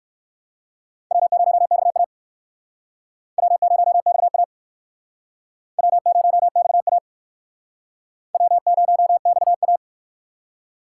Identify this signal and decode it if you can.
Morse code, heard as a keyed tone, about 59 wpm, 700 Hz. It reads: W0XA W0XA W0XA W0XA